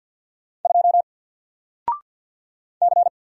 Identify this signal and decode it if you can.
Morse code, heard as a keyed tone, about 48 wpm, 700 Hz. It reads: 2 C
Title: Single Character (Letter:Number:Punctuation:Prosign) Alt 50wpm